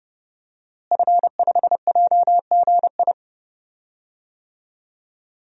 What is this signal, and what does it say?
Morse code, heard as a keyed tone, about 32 wpm, 700 Hz. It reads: F5JGI
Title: DXCC-100 Call Signs - 5 Characters - Repeated 2x - 30wpm